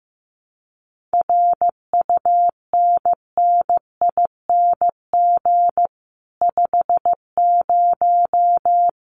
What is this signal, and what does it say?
Morse code, heard as a keyed tone, about 15 wpm, 700 Hz. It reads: RUNNING 50
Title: Common QSO Phrases 15wpm